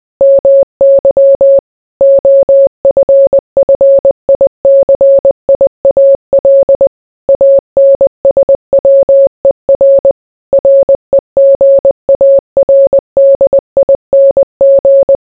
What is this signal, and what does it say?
Morse code, heard as a keyed tone, about 20 wpm, 550 Hz. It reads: MY OFFICIAL ANSWER REGARDING